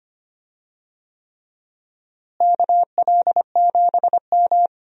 Morse code, heard as a keyed tone, about 25 wpm, 700 Hz. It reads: KL7M
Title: DXCC-100 Call Signs - 4 Characters - Repeated 3x - 25wpm